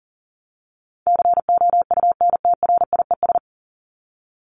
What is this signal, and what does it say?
Morse code, heard as a keyed tone, about 40 wpm, 700 Hz. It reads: COUNTRIES
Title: ES:S Suffix - Top 100 Words with Base Words 40wpm